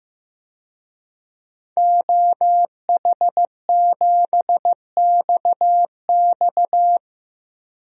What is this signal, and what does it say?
Morse code, heard as a keyed tone, about 15 wpm, 700 Hz. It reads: OH7XX
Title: DXCC-100 Call Signs - 5 Characters - Repeated 2x - 15wpm